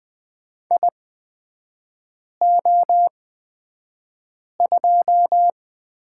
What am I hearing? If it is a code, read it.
Morse code, heard as a keyed tone, about 20 wpm, spaced as for 5 wpm, 700 Hz. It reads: IO2
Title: 243 Lesson 39e - Letter z - Three Characters 20wpm